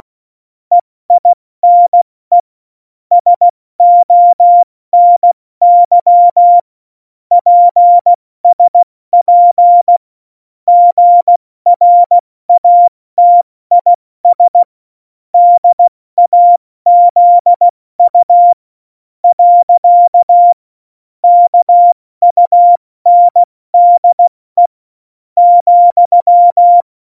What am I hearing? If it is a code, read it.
Morse code, heard as a keyed tone, about 16 wpm, spaced as for 13 wpm, 700 Hz. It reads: EINE SONY PSP GRATIS DAZU. KUNDE,
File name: Morsekurs / Witze / Witz 15. Januar 2019, 70 F80